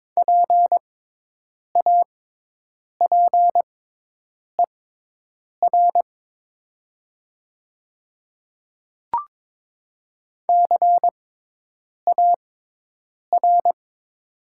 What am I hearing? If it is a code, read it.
Morse code, heard as a keyed tone, about 22 wpm, spaced as for 8 wpm, 700 Hz. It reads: PAPER CAR
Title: 6x Character Spacing - Top 300 Words - 22wpm